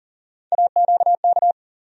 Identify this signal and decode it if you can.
Morse code, heard as a keyed tone, about 40 wpm, 700 Hz. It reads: AQK
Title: USA Call Sign Postfixes - 3 Letters 40wpm